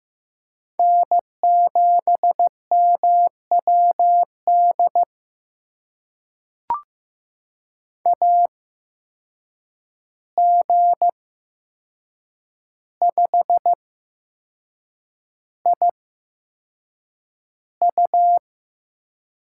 Morse code, heard as a keyed tone, about 15 wpm, spaced as for 6 wpm, 700 Hz. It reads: N7MWD AG5IU
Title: USA Call Signs - 5 Characters - 8x Character Spacing - 15wpm